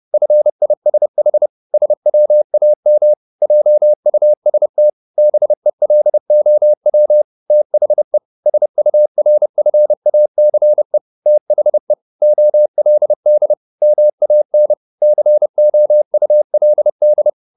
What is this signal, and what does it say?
Morse code, heard as a keyed tone, about 30 wpm, 600 Hz. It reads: FISH SWAM JUST BELOW THE SURFACE THE OLD MAN COULD